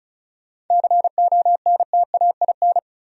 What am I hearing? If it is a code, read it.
Morse code, heard as a keyed tone, about 45 wpm, 700 Hz. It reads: CONTAIN